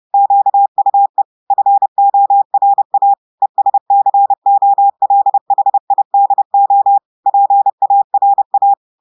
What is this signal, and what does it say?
Morse code, heard as a keyed tone, about 30 wpm, 800 Hz. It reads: QUE FORA ESCOLHIDO PARA
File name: audios / portugais / portugais texte en clair 1 vitesse 30 mots minutes